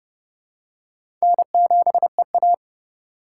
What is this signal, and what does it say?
Morse code, heard as a keyed tone, about 30 wpm, 700 Hz. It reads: N7EA
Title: USA Call Signs - 3x Character Spacing - 30wpm